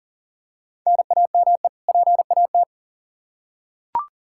Morse code, heard as a keyed tone, about 40 wpm, 700 Hz. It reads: NAME PAT
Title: Practice QSO - Exchange 1 - Beginning - 40wpm